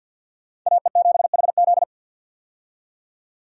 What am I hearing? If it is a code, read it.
Morse code, heard as a keyed tone, about 51 wpm, 700 Hz. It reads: AE7HZ
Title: USA Call Signs - 5 Characters - 50wpm